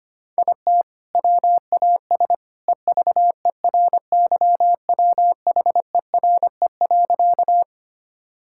Morse code, heard as a keyed tone, about 25 wpm, 700 Hz. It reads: IT WAS EVERYWHERE.
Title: Sentences from Top 1600 Words 25wpm